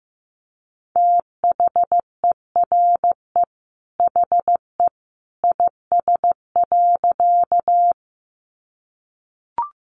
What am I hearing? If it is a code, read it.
Morse code, heard as a keyed tone, about 15 wpm, 700 Hz. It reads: THERE HE IS.
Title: A Sound Of Thunder 15wpm